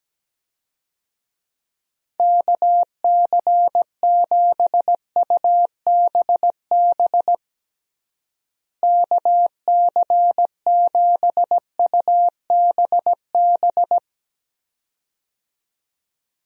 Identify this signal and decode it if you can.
Morse code, heard as a keyed tone, about 17 wpm, 700 Hz. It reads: KC7UBB KC7UBB
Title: USA Call Signs - 6 Characters - Repeated 2x - 17wpm